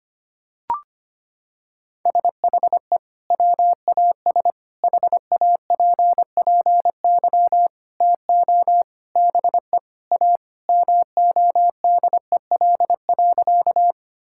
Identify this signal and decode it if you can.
Morse code, heard as a keyed tone, about 25 wpm, 700 Hz. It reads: SHE WAS HAPPY TO BE A MODEL.
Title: Sentences from Top 1000 Words - Review Part 5 - 25wpm